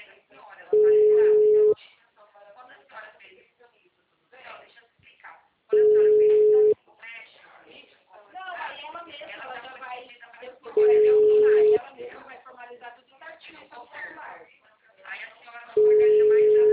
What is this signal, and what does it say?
Ringback tone (Europe, 425 Hz), with speech.